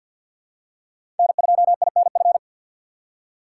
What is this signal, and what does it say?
Morse code, heard as a keyed tone, about 53 wpm, 690 Hz. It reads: N2INF
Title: USA Call Signs - 5 Characters - 6x Character Spacing - 50wpm